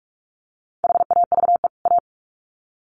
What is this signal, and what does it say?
Morse code, heard as a keyed tone, about 44 wpm, 700 Hz. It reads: HAVE A
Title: Sets of 2 Words 45wpm